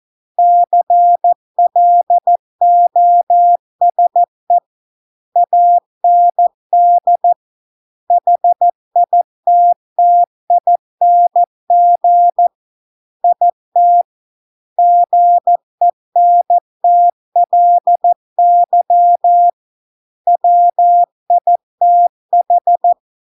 Morse code, heard as a keyed tone, about 14 wpm, 700 Hz. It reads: CLOSE AND HITTING IT GENTLY WITH